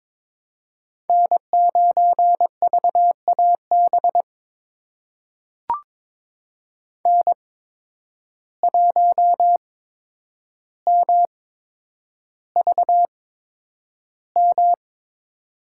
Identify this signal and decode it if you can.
Morse code, heard as a keyed tone, about 22 wpm, spaced as for 9 wpm, 700 Hz. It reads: N9VAB N1MVM
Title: USA Call Signs - 8x Character Spacing - 22wpm